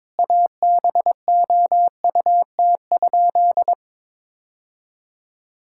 Morse code, heard as a keyed tone, about 22 wpm, 700 Hz. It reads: ABOUT?
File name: practice / 22wpm / Sentences with 2-Word Phrases - 22wpm